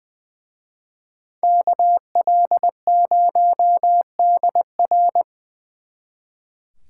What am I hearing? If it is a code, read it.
Morse code, heard as a keyed tone, about 22 wpm, 700 Hz. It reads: KL0DR
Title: Speed Racing - USA 2x2 Call Signs 20wpm